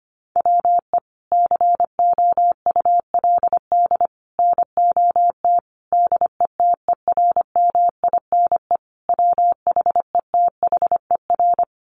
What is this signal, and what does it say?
Morse code, heard as a keyed tone, about 25 wpm, 700 Hz. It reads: WE COULD NOT DETERMINE WHETHER
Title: Sentences from Top 1300 Words 25wpm